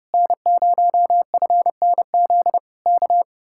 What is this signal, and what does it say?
Morse code, heard as a keyed tone, about 30 wpm, 700 Hz. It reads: N0FNZ K